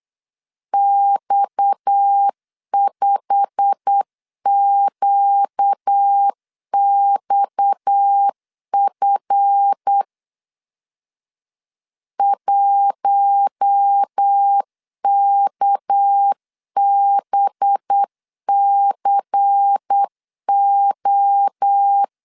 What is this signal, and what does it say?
Morse code, heard as a keyed tone, about 8 wpm, 800 Hz. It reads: X5QXF 1KBCO
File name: CW / DET / 1 9 1500C S B 8wpm 5624